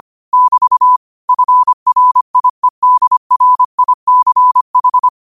Morse code, heard as a keyed tone, about 25 wpm, 1000 Hz. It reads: <BT> FRIEDRICH